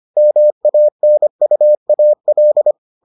Morse code, heard as a keyed tone, about 25 wpm, 600 Hz. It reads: MANUAL